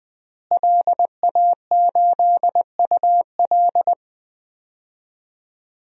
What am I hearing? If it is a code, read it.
Morse code, heard as a keyed tone, about 20 wpm, 700 Hz. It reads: LA8UL
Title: DXCC-100 Call Signs - 5 Characters - Repeated 2x - 20wpm